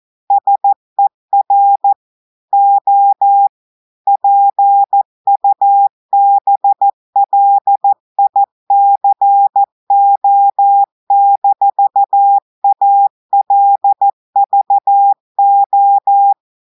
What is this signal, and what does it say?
Morse code, heard as a keyed tone, about 14 wpm, 800 Hz. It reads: SER O PUBLICO-ALVO
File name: audios / portugais / portugais texte en clair 2 vitesse 14 mots minutes